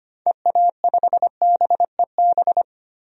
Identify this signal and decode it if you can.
Morse code, heard as a keyed tone, about 25 wpm, 700 Hz. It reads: EA5BEB
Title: DXCC-100 Call Signs - 6 Characters - 25wpm